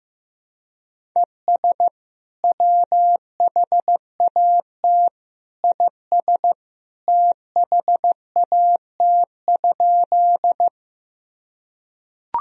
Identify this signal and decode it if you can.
Morse code, heard as a keyed tone, about 16 wpm, 700 Hz. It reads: ES WHAT IS THAT?
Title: Sentences from Top 100 Words and Abbreviations 15wpm